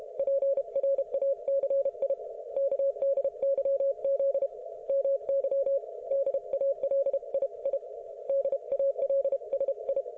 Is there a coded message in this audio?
Yes: Morse code from 0.2 to 10.0 s, heard as a keyed tone, about 32 wpm, 550 Hz. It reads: PRACI KDYZ MY DALI I DALSI